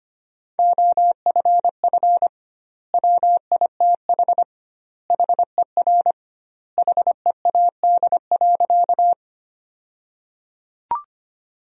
Morse code, heard as a keyed tone, about 25 wpm, 700 Hz. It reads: OFF WITH HER HEAD.
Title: 2x Word-Spacing - Sentences from Top 300 Words - 25wpm